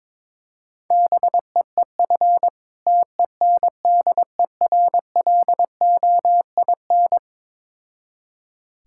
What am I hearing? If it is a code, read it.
Morse code, heard as a keyed tone, about 22 wpm, 690 Hz. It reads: BEEF TENDERLOIN
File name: Difficult Words with Hints - 22wpm